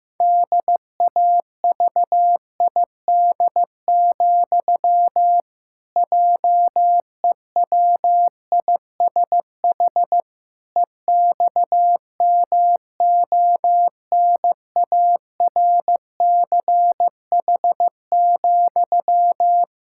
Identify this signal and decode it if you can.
Morse code, heard as a keyed tone, about 15 wpm, 700 Hz. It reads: DAVID, JEWISH EXMONARCH,
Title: Pangram Challenge v2 - 15wpm